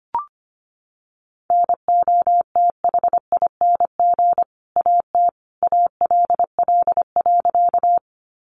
Morse code, heard as a keyed tone, about 25 wpm, 700 Hz. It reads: NOTHING AT ALL.
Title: Sentences from Top 500 Words 25wpm